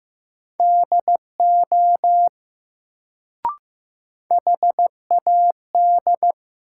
Morse code, heard as a keyed tone, about 16 wpm, 700 Hz. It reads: DO HAD
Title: Sentences from Top 100 Words - New Word Drill - 15wpm